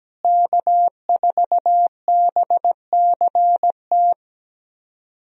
Morse code, heard as a keyed tone, about 17 wpm, 700 Hz. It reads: K4BCT
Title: USA Call Signs - 5 Characters - Repeated 3x - 17wpm